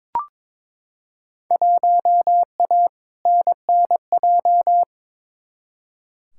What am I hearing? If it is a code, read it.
Morse code, heard as a keyed tone, about 22 wpm, 700 Hz. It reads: 1A NNJ
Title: Field Day Exchanges Encore - 22wpm